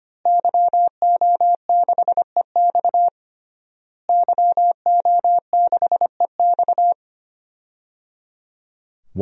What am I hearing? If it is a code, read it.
Morse code, heard as a keyed tone, about 25 wpm, 700 Hz. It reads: YO6EX YO6EX
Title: DXCC-100 Call Signs - 5 Characters - Repeated 3x - 25wpm